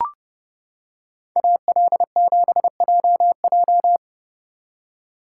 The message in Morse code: AL7JJ

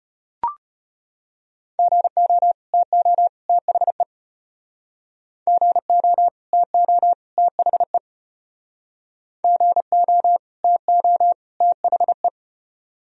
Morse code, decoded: GO TO THE GO TO THE GO TO THE

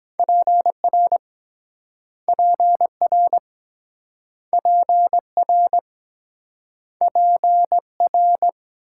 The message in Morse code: PR PR PR PR

